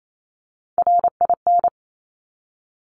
Morse code, heard as a keyed tone, about 28 wpm, 700 Hz. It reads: RIN
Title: Top Three Letter Combinations in Words 28wpm